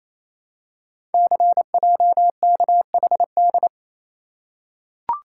Morse code, heard as a keyed tone, about 28 wpm, 700 Hz. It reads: CJKHD
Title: Code Groups - 5 Letter:Numbers - 28wpm